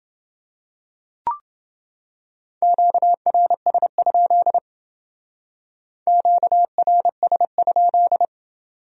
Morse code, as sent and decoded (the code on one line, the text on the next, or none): --.- .-. ... ..--.. / --.- .-. ... ..--..
QRS? QRS?